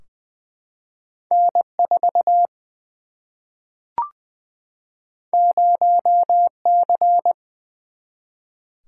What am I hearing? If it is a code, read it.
Morse code, heard as a keyed tone, about 20 wpm, 700 Hz. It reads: N 4 0 C
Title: Single-Letter Single-Number 20wpm